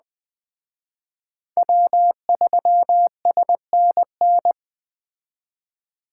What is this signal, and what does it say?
Morse code, heard as a keyed tone, about 21 wpm, 700 Hz. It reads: W3SNN